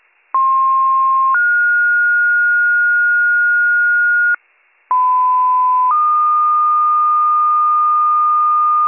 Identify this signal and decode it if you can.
Two-tone page, heard as a one-second tone, then a three-second tone, about 1060 then 1430 Hz, about 1010 then 1220 Hz, with noise. It is heard from 0.3 to 8.9 s.